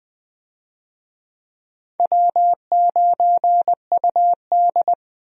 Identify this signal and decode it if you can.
Morse code, heard as a keyed tone, about 21 wpm, 700 Hz. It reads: W9UD
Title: USA Call Signs - 4 Characters - Repeated 3x - 20wpm